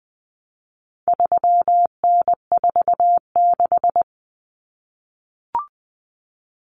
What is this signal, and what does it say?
Morse code, heard as a keyed tone, about 20 wpm, 700 Hz. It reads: 3N46
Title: Common Cut Numbers - 4-Digit 20wpm